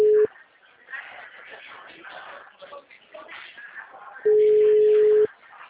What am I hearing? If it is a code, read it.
Ringback tone (Europe, 425 Hz).
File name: out- -2005- - - .99573